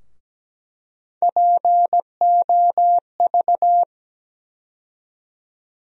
Morse code, heard as a keyed tone, about 17 wpm, 700 Hz. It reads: POV
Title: USA Call Sign Postfixes - 3 Letters 17wpm